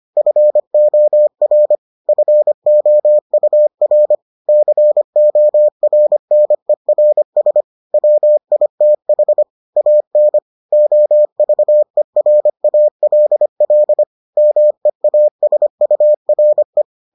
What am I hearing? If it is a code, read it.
Morse code, heard as a keyed tone, about 25 wpm, 600 Hz. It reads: FOR FOUR CORNERS WITH AN OVERALL MEASURE